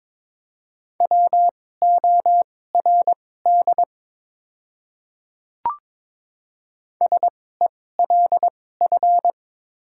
Morse code, heard as a keyed tone, about 21 wpm, spaced as for 16 wpm, 700 Hz. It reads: WORD SELF
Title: 2x Character Spacing - Top 200 Words - 22wpm